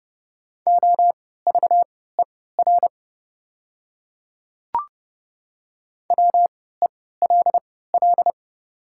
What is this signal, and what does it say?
Morse code, heard as a keyed tone, about 30 wpm, spaced as for 17 wpm, 690 Hz. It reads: OVER WELL
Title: 3x Character Spacing - Top 200 Words - 30wpm